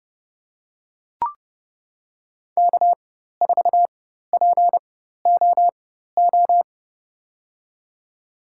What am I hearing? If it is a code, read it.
Morse code, heard as a keyed tone, about 31 wpm, spaced as for 14 wpm, 700 Hz. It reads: K4POO
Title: USA Call Signs - 5 Characters - 4x Character Spacing - 30wpm